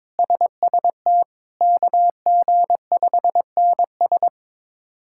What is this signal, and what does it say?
Morse code, heard as a keyed tone, about 22 wpm, 700 Hz. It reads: SST KG5NS
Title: K1USN SST QSO Practice - 22wpm